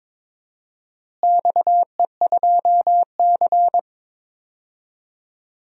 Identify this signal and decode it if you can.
Morse code, heard as a keyed tone, about 21 wpm, 700 Hz. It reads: XE2C
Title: DXCC-100 Call Signs - 4 Characters - Repeated 3x - 22wpm